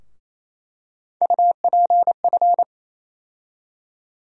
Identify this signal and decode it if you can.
Morse code, heard as a keyed tone, about 28 wpm, 700 Hz. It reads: UPF
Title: USA Call Sign Postfixes - 3 Letters 28wpm